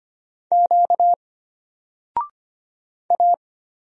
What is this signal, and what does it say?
Morse code, heard as a keyed tone, about 25 wpm, 700 Hz. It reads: Q A
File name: Single-Letter-Number-25wpm